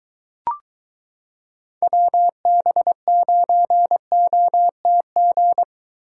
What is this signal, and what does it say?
Morse code, heard as a keyed tone, about 22 wpm, 700 Hz. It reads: WB9OTG